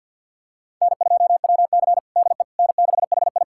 Morse code, heard as a keyed tone, about 52 wpm, 700 Hz. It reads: N2WC DE N6HI
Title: USA Callsign DE Callsign - 4 Characters - 50wpm